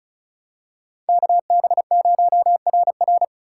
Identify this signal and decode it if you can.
Morse code, heard as a keyed tone, about 36 wpm, 700 Hz. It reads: KB0RR
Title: USA Call Signs - 5 Characters - 3x Character Spacing - 35wpm